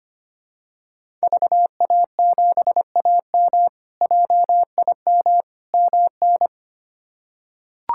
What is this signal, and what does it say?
Morse code, heard as a keyed tone, about 25 wpm, 700 Hz. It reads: VA7AM JIM MN
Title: K1USN SST QSO Practice - 25wpm